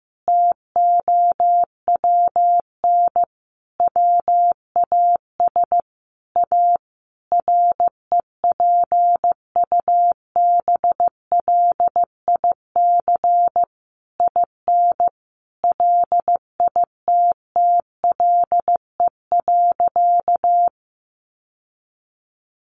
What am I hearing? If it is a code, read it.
Morse code, heard as a keyed tone, about 15 wpm, 700 Hz. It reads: TOWN WAS A REPUBLIC IN LITTLE.
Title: Sentences from Top 1500 Words 15wpm